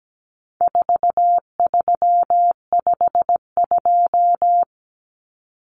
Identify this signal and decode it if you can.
Morse code, heard as a keyed tone, about 17 wpm, 700 Hz. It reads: 4352
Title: Common Cut Numbers - 4-Digit 17wpm